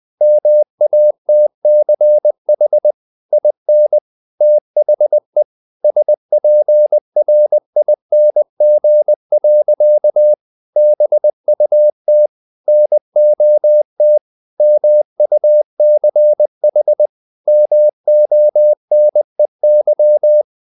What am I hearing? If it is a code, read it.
Morse code, heard as a keyed tone, about 20 wpm, 600 Hz. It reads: MATCH IN THE SPRING. BUT NOT MUCH MONEY